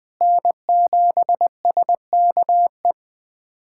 Morse code, heard as a keyed tone, about 20 wpm, 700 Hz. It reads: N7SKE
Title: USA Callsign DE Callsign - 5 Characters - 20wpm